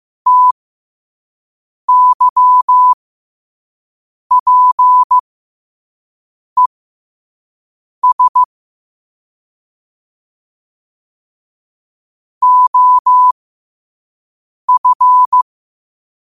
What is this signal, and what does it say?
Morse code, heard as a keyed tone, about 15 wpm, spaced as for 5 wpm, 1000 Hz. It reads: TYPES OF
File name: qotdfiles / 5wpm